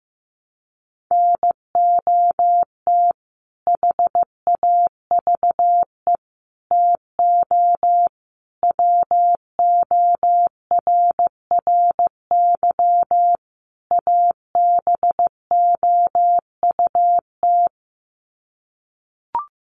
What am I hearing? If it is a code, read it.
Morse code, heard as a keyed tone, about 15 wpm, 700 Hz. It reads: NOT HAVE TO WORRY ABOUT
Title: Sets of 5 Words 15wpm